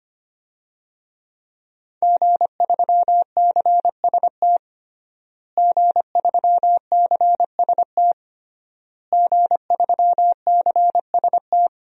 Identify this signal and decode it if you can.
Morse code, heard as a keyed tone, about 25 wpm, 700 Hz. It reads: G3CST G3CST G3CST
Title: DXCC-100 Call Signs - 5 Characters - Repeated 3x - 25wpm